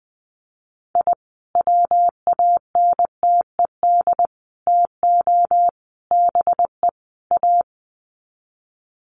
Morse code, heard as a keyed tone, about 20 wpm, 700 Hz. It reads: I WANTED TO BE A
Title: Sets of 5 Words Encore2 20wpm